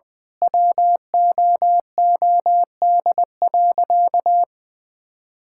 Morse code, heard as a keyed tone, about 20 wpm, 700 Hz. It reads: WOOD.
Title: Sentences from Top 1600 Words 20wpm